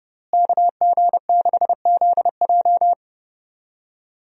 Morse code, heard as a keyed tone, about 30 wpm, 700 Hz. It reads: KG6ZJ
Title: USA Call Signs - 5 Characters - Repeated 3x - 30wpm